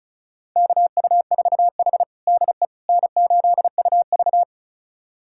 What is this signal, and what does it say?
Morse code, heard as a keyed tone, about 34 wpm, 700 Hz. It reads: KU4H DE N8UV
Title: USA Callsign DE Callsign - 4 Characters - 35wpm